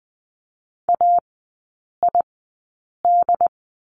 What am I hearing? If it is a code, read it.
Morse code, heard as a keyed tone, about 19 wpm, spaced as for 8 wpm, 700 Hz. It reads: AID
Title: 067 Lesson 12e - Letter d - Three Characters 20wpm